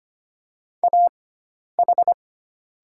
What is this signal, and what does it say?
Morse code, heard as a keyed tone, about 25 wpm, spaced as for 10 wpm, 700 Hz. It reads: AH
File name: Lesson 16g - Letter c - Call Signs 25wpm